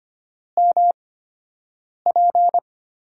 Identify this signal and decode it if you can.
Morse code, heard as a keyed tone, about 25 wpm, spaced as for 7 wpm, 700 Hz. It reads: MP
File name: Top 90 QSO Elements - 8x Character Spacing - 25wpm